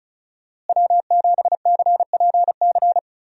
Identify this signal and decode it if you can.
Morse code, heard as a keyed tone, about 34 wpm, 700 Hz. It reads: W7CPC
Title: USA Call Signs - 5 Characters - Repeated 2x - 35wpm